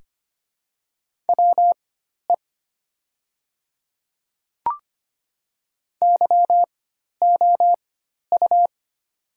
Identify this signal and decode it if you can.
Morse code, heard as a keyed tone, about 25 wpm, spaced as for 12 wpm, 700 Hz. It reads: WE YOU
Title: 4x Character Spacing - Top 100 Words - 25wpm